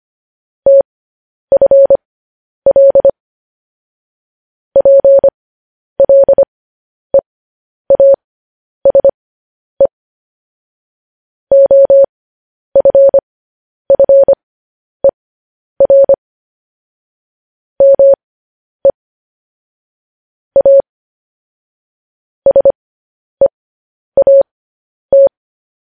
Morse code, heard as a keyed tone, about 25 wpm, spaced as for 10 wpm, 550 Hz. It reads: TFL PLEASE OFFER ME A SEAT
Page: news headlines in morse code at 10 wpm